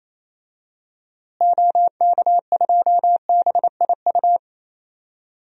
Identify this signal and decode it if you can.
Morse code, heard as a keyed tone, about 28 wpm, 700 Hz. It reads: OK2BIU
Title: DXCC-100 Call Signs - 6 Characters - Repeated 2x - 28wpm